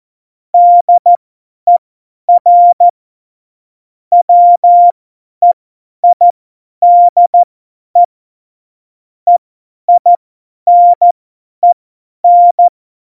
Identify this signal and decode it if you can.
Morse code, heard as a keyed tone, about 14 wpm, spaced as for 10 wpm, 700 Hz. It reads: DER WEIDE EINEN